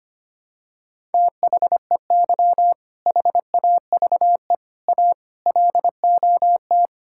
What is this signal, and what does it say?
Morse code, heard as a keyed tone, about 25 wpm, 700 Hz. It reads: THEY HAVE A LOT
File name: practice / 25wpm / Sentences with 4-Word Phrases - 25wpm